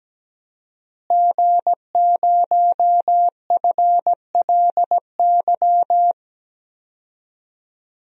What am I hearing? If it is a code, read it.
Morse code, heard as a keyed tone, about 17 wpm, 700 Hz. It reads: G0FLY
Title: DXCC-100 Call Signs - 5 Characters - Repeated 3x - 17wpm